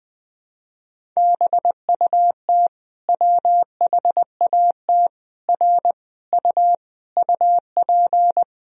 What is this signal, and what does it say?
Morse code, heard as a keyed tone, about 20 wpm, 700 Hz. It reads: BUT WHAT R U UP
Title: Sentences from Top 100 Words and Abbreviations 20wpm